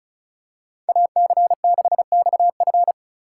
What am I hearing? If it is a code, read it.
Morse code, heard as a keyed tone, about 34 wpm, 700 Hz. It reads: AC6XF
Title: USA Call Signs - 5 Characters - 4x Character Spacing - 35wpm